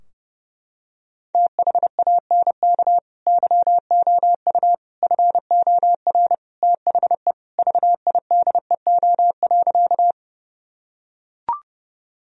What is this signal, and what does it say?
Morse code, heard as a keyed tone, about 32 wpm, 700 Hz. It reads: THANK YOU FOR THE VIDEO.
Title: Sentences from Top 1000 Words - Review Part 4 - 30wpm